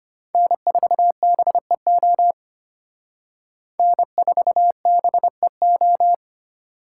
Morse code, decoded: N4BEO N4BEO

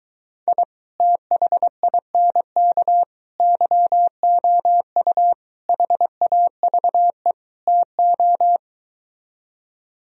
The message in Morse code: I THINK YOU HAVE TO

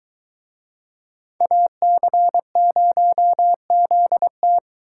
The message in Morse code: AC0ZT

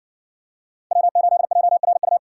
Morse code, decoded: W8JRU

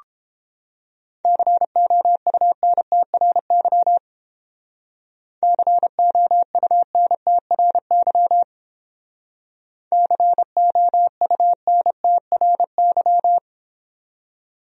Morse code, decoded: COUNTRY COUNTRY COUNTRY

